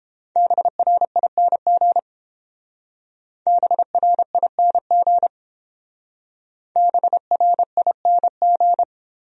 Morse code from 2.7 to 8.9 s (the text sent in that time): BRING BRING